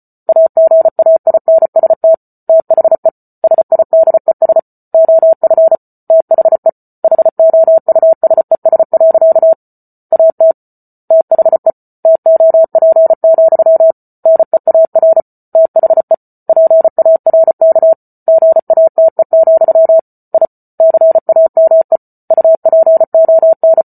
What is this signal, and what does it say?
Morse code, heard as a keyed tone, about 34 wpm, 650 Hz. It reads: AGAINST THE SIDES OF THE HOUSES. AT THE TOP, NEAR THE PARK GATE, I CAME UPON